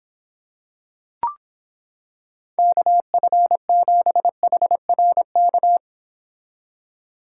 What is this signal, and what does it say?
Morse code, heard as a keyed tone, about 26 wpm, 700 Hz. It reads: KF7HRK